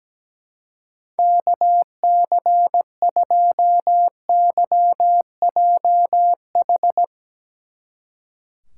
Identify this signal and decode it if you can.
Morse code, heard as a keyed tone, about 20 wpm, 700 Hz. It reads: KC2YJH
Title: Speed Racing - USA 2x3 Call Signs 17wpm